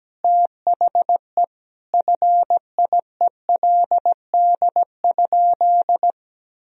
Morse code, heard as a keyed tone, about 17 wpm, 700 Hz. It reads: THE FIELD?
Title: Sentences from Top 1000 Words - Review Part 2 - 17wpm